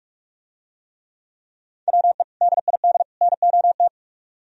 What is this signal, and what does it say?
Morse code, heard as a keyed tone, about 44 wpm, 700 Hz. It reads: WE DID NOT